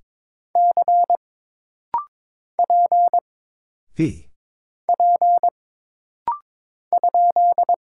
Morse code, heard as a keyed tone, about 22 wpm, 700 Hz. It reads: C P P ?
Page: This rapid-fire timing is geared to push your ICR (Instant Character Recognition) skills.